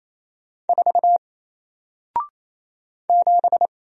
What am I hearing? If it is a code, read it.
Morse code, heard as a keyed tone, about 27 wpm, 700 Hz. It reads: 4 7
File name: Single-Letter-Number-28wpm